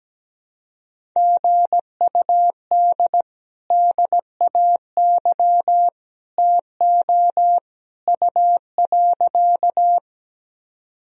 Morse code, heard as a keyed tone, about 17 wpm, 700 Hz. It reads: GUD DAY TO U.
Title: Sentences from Top 200 Words and Abbreviations 17wpm